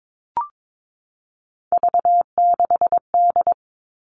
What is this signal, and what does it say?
Morse code, heard as a keyed tone, about 22 wpm, 700 Hz. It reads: V6D
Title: All Cut Numbers - 3-Digit 22wpm